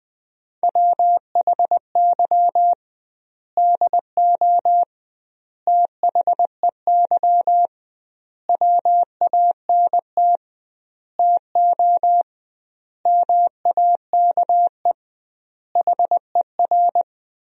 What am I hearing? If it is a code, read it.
Morse code, heard as a keyed tone, about 20 wpm, 700 Hz. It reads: WHY DO THEY WANT TO MAKE HER
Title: 2x Word-Spacing - Sentences from Top 400 Words - 20wpm